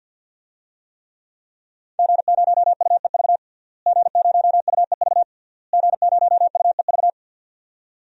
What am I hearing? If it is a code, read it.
Morse code, heard as a keyed tone, about 51 wpm, 700 Hz. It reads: G0UEV G0UEV G0UEV
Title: DXCC-100 Call Signs - 5 Characters - Repeated 3x - 50wpm